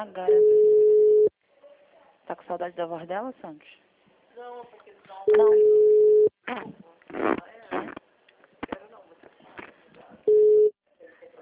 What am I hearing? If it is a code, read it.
Ringback tone (Europe, 425 Hz), with speech.